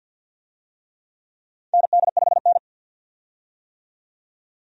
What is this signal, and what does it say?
Morse code, heard as a keyed tone, about 47 wpm, 700 Hz. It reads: ND5N
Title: USA Call Signs - 4 Characters - 50wpm